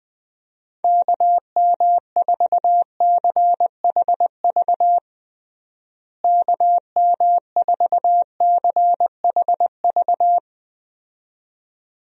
Morse code, heard as a keyed tone, about 20 wpm, 700 Hz. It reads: KM4CHV KM4CHV
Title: USA Call Signs - 6 Characters - Repeated 2x - 20wpm